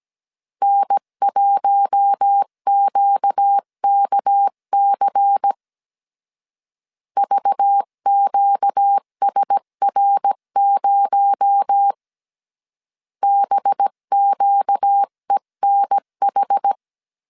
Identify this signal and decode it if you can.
Morse code, heard as a keyed tone, about 17 wpm, 800 Hz. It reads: N1QKC VQSR0 BQENH
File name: CW / DET / 1 6 1500C S C 16wpm 2733